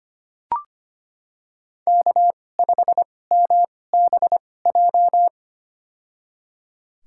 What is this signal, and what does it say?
Morse code, heard as a keyed tone, about 25 wpm, spaced as for 18 wpm, 700 Hz. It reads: K5MBJ
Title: USA Call Signs - 5 Characters - 2x Character Spacing - 25wpm